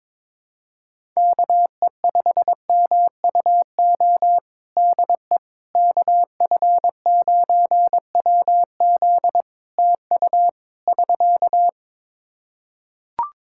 Morse code, heard as a keyed tone, about 22 wpm, 700 Hz. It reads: KE5MUO DE KF9WZ TU <SK>
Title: K1USN SST QSO Practice 22wpm